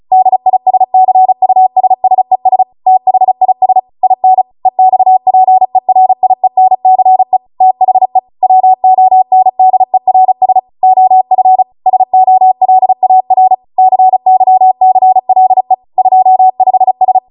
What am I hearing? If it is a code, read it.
Morse code, heard as a keyed tone, about 36 wpm, 750 Hz. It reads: DISCUSSES THIS IN EXPERIENCE THE WONDERS OF SOLAR CYCLE 25S